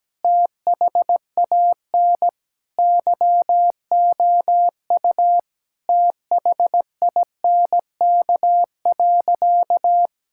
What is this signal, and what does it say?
Morse code, heard as a keyed tone, about 17 wpm, 700 Hz. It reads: THAN YOU THINK.
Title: Sentences from Top 1000 Words - Review Part 6 - 17wpm